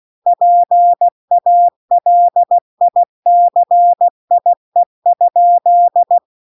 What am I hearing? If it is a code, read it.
Morse code, heard as a keyed tone, about 16 wpm, 700 Hz. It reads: PALICIE?
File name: recordings / 16wpm0000